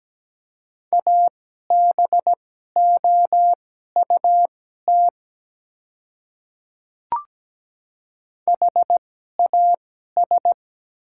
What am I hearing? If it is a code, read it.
Morse code, heard as a keyed tone, about 17 wpm, spaced as for 12 wpm, 700 Hz. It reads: ABOUT HAS
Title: 2x Character Spacing - Top 100 Words - 17wpm